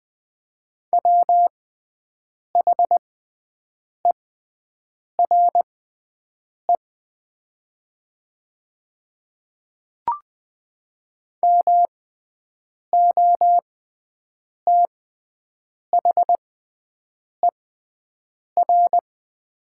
Morse code, heard as a keyed tone, about 20 wpm, spaced as for 7 wpm, 700 Hz. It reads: WHERE MOTHER
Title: 6x Character Spacing - Top 200 Words - 20wpm